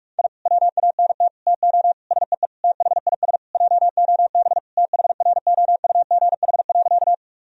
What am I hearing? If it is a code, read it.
Morse code, heard as a keyed tone, about 46 wpm, 700 Hz. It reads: I WANT TO SEE THIS JOB THROUGH.
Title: Sentences from Top 1000 Words - Review Part 3 - 45wpm